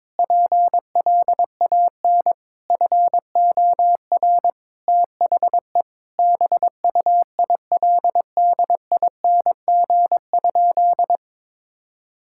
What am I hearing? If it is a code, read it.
Morse code, heard as a keyed tone, about 22 wpm, 700 Hz. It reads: PLAN FOR THE BUILDING?
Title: Sentences from Top 1000 Words - Review Part 5 - 22wpm